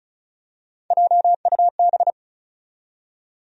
Morse code, heard as a keyed tone, about 36 wpm, 700 Hz. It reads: JUB
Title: USA Call Sign Postfixes - 3 Letters 35wpm